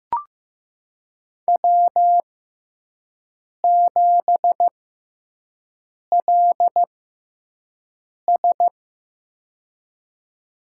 Morse code, heard as a keyed tone, about 15 wpm, spaced as for 5 wpm, 700 Hz. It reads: W7LS
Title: USA Call Signs - 4 Characters - 6x Character Spacing - 15wpm